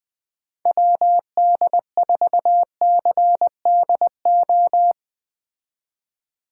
Morse code, heard as a keyed tone, about 20 wpm, 700 Hz. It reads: WD4CDO
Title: Speed Racing - USA 2x3 Call Signs 17wpm